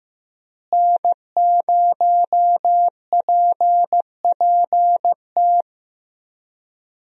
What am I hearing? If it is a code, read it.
Morse code, heard as a keyed tone, about 15 wpm, 700 Hz. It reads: N0PPT
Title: USA Call Signs - 5 Characters - Repeated 3x - 15wpm